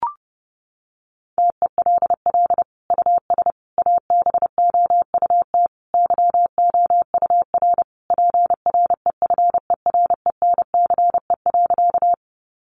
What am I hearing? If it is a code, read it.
Morse code, heard as a keyed tone, about 30 wpm, 700 Hz. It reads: TELL US ABOUT YOUR PREFERENCE.
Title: Sentences from Top 1500 Words 30wpm